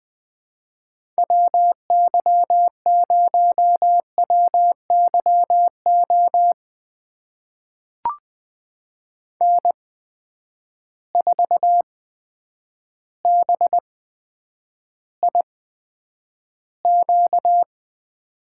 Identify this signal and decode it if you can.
Morse code, heard as a keyed tone, about 20 wpm, spaced as for 9 wpm, 700 Hz. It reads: WY0WYO N4BIQ
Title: USA Call Signs - 8x Character Spacing - 20wpm